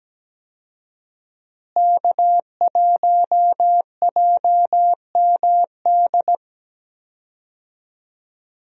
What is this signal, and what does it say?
Morse code, heard as a keyed tone, about 17 wpm, 700 Hz. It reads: K1JMD
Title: USA Call Signs - 5 Characters - Repeated 2x - 17wpm